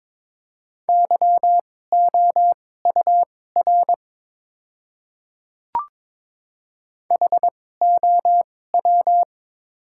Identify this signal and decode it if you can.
Morse code, heard as a keyed tone, about 22 wpm, spaced as for 16 wpm, 690 Hz. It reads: YOUR HOW
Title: 2x Character Spacing - Top 200 Words - 22wpm